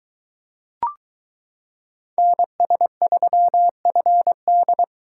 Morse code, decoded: NS3FD

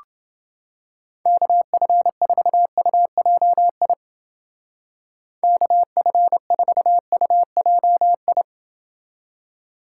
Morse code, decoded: KF4UJI KF4UJI

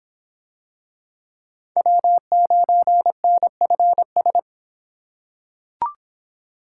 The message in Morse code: W9NFS

